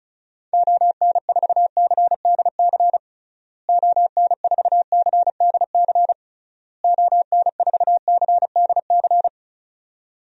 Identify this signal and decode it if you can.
Morse code, heard as a keyed tone, about 36 wpm, 700 Hz. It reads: ON4CDC ON4CDC ON4CDC
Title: DXCC-100 Call Signs - 6 Characters - Repeated 3x - 35wpm